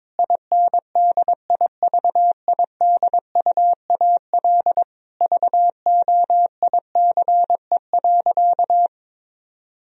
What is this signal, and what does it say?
Morse code, heard as a keyed tone, about 22 wpm, 700 Hz. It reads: INDIVIDUAL VOICE.
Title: Sentences from Top 1000 Words - Review Part 4 - 22wpm